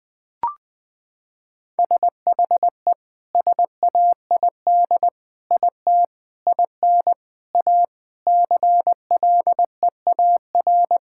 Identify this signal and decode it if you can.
Morse code, heard as a keyed tone, about 20 wpm, 700 Hz. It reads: SHE SAID IT IN A CLEAR
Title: Sentences from Top 1600 Words 20wpm